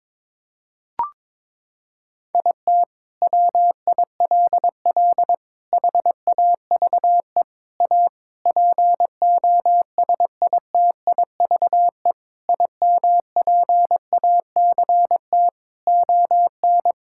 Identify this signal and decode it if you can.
Morse code, heard as a keyed tone, about 22 wpm, 700 Hz. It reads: IT WILL HAVE A POSITIVE IMPACT ON
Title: Sentences from Top 1000 Words - Review Part 6 - 22wpm